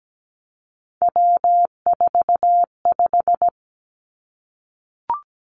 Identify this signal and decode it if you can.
Morse code, heard as a keyed tone, about 17 wpm, 700 Hz. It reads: W45
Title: Three Challenging Letter-Number Combinations 17wpm